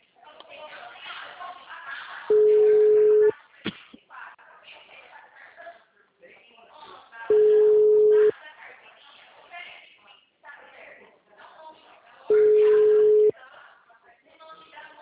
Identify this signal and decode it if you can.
Ringback tone (Europe, 425 Hz), with speech.
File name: out- -2010- - - .69843